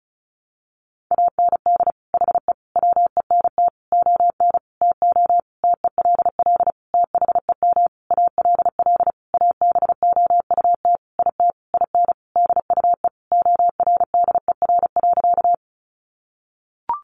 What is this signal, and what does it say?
Morse code, heard as a keyed tone, about 34 wpm, 700 Hz. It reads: AND HE WENT ON TO TELL THEM ALL ABOUT IT IN DUE ORDER.
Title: Sentences from Top 900 Words 35wpm